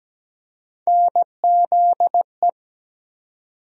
Morse code, heard as a keyed tone, about 17 wpm, 700 Hz. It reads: NZE
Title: USA Call Sign Postfixes - 3 Letters 17wpm